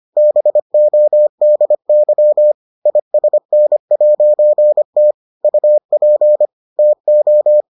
Morse code, heard as a keyed tone, about 25 wpm, 600 Hz. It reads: BODY ISN'T UP TO